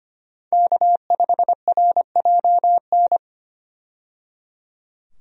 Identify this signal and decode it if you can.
Morse code, heard as a keyed tone, about 25 wpm, 700 Hz. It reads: K5RJN
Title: USA Call Signs - 5 Characters - 25wpm